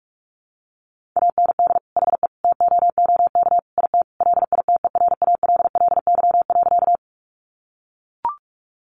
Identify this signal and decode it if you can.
Morse code, heard as a keyed tone, about 44 wpm, 700 Hz. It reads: AND HE TOOK IT LITERALLY.
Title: Sentences from Top 1400 Words 45wpm